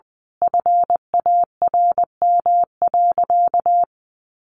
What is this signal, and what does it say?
Morse code, heard as a keyed tone, about 20 wpm, 700 Hz. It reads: FARM.
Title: Sentences from Top 1200 Words 20wpm